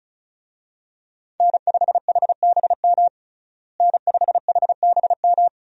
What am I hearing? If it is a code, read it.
Morse code, heard as a keyed tone, about 36 wpm, 700 Hz. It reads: N5HBM N5HBM
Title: USA Call Signs - 5 Characters - Repeated 3x - 35wpm